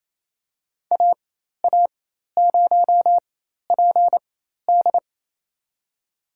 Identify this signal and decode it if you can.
Morse code, heard as a keyed tone, about 28 wpm, spaced as for 13 wpm, 700 Hz. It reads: AA0PD
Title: USA Call Signs - 5 Characters - 4x Character Spacing - 28wpm